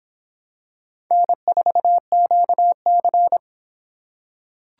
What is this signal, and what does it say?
Morse code, heard as a keyed tone, about 26 wpm, 700 Hz. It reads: N4QC